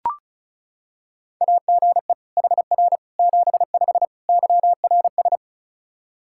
Morse code, heard as a keyed tone, about 36 wpm, 700 Hz. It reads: AGE HR 75 YRS
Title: K1USN SST QSO Practice 35wpm